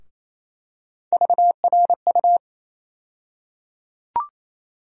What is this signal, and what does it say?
Morse code, heard as a keyed tone, about 28 wpm, 700 Hz. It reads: VRU
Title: USA Call Sign Postfixes - 3 Letters 28wpm